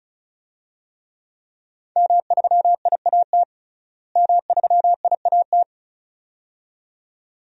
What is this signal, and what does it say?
Morse code, heard as a keyed tone, about 36 wpm, 700 Hz. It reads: M3IAT M3IAT
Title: DXCC-100 Call Signs - 5 Characters - Repeated 2x - 35wpm